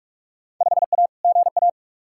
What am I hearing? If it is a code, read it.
Morse code, heard as a keyed tone, about 44 wpm, 700 Hz. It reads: 5A GA
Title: Field Day Exchanges Encore - 45wpm